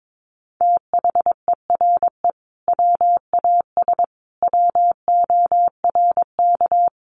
Morse code, heard as a keyed tone, about 22 wpm, 700 Hz. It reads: THERE WAS WORK
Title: Sentences from Top 200 Words 22wpm